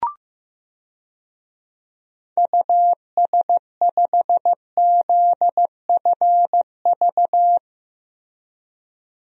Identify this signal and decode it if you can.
Morse code, heard as a keyed tone, about 15 wpm, 700 Hz. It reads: US5ZFV
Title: DXCC-100 Call Signs - 6 Characters - Repeated 3x - 15wpm